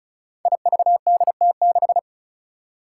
Morse code, heard as a keyed tone, about 36 wpm, 700 Hz. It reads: IVDT6
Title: Five Challenging Letter-Number Combinations 35wpm